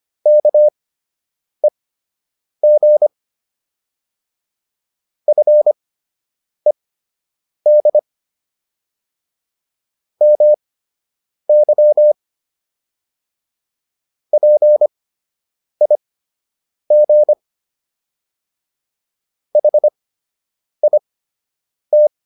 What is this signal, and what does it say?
Morse code, heard as a keyed tone, about 25 wpm, spaced as for 8 wpm, 600 Hz. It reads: KEG FED MY PIG HIT